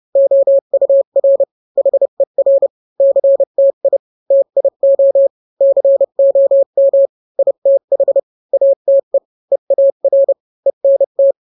Morse code, heard as a keyed tone, about 30 wpm, 550 Hz. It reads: OUR HER CTI TIO COM ITH ATE EAR ENT